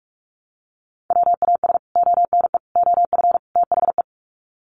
Morse code, heard as a keyed tone, about 44 wpm, 700 Hz. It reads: WAS ONE OF THE
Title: Sets of 4 Words 45wpm